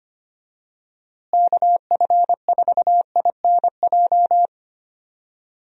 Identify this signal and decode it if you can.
Morse code, heard as a keyed tone, about 25 wpm, 700 Hz. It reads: KF4INJ
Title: USA Call Signs - 6 Characters - Repeated 3x - 25wpm